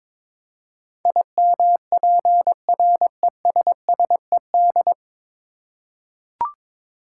Morse code, heard as a keyed tone, about 22 wpm, 700 Hz. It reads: IMPRESSED
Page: Morse Code Ninja practice 22wpm